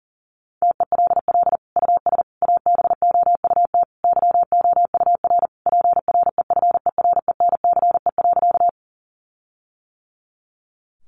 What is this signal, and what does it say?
Morse code, heard as a keyed tone, about 40 wpm, 700 Hz. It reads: TELL US ABOUT YOUR PREFERENCE.
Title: Sentences from Top 1500 Words 40wpm